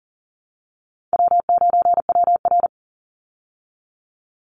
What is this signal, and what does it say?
Morse code, heard as a keyed tone, about 40 wpm, 700 Hz. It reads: W9WR
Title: Easy Call Signs Repeated 3x - 40wpm